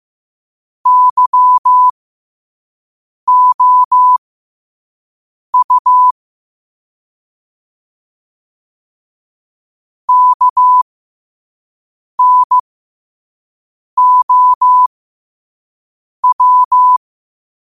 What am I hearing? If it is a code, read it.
Morse code, heard as a keyed tone, about 15 wpm, spaced as for 5 wpm, 1000 Hz. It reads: YOU KNOW
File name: qotdfiles / 5wpm